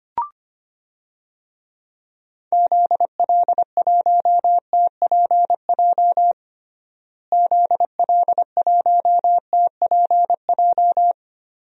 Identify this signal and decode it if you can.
Morse code, heard as a keyed tone, about 25 wpm, 700 Hz. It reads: ZL1TPJ ZL1TPJ
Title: DXCC-100 Call Signs - 6 Characters - Repeated 3x - 25wpm